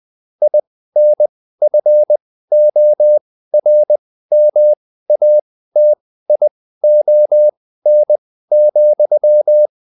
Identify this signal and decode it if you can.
Morse code, heard as a keyed tone, about 20 wpm, spaced as for 14 wpm, 600 Hz. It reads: INFORMATION,